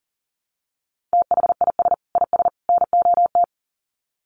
Morse code, heard as a keyed tone, about 40 wpm, 700 Hz. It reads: THIS IS NOT
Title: Sets of 3 Words 40wpm